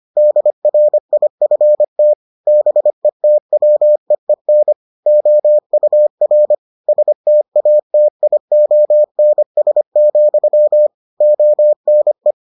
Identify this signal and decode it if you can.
Morse code, heard as a keyed tone, about 25 wpm, 600 Hz. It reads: DRIFT BETWEEN OUR STATIONS, ONE